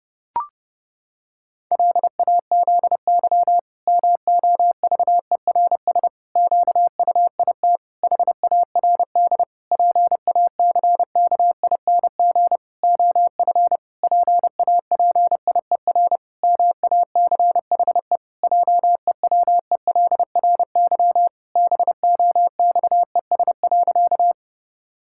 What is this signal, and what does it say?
Morse code, heard as a keyed tone, about 30 wpm, 700 Hz. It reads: LAZY MOVERS QUIT HARD PACKING OF PAPIER MACHE JEWELRY BOXES.
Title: Pangram Challenge v2 - 30wpm